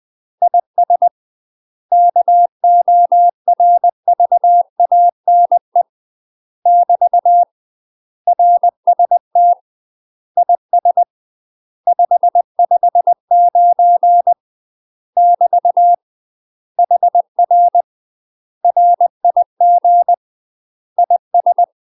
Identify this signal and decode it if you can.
Morse code, heard as a keyed tone, about 20 wpm, 700 Hz. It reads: IS KORVANE <BT> RST IS 559 <BT> HR RIG IS
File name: UK MASTER QSOs TRAINING / 20WPM / UK MASTER QSOs 0021